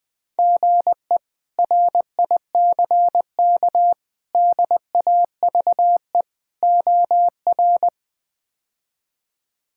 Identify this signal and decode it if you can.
Morse code, heard as a keyed tone, about 20 wpm, 700 Hz. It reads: GE RICK DAVE OR
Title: K1USN SST QSO Practice - 20wpm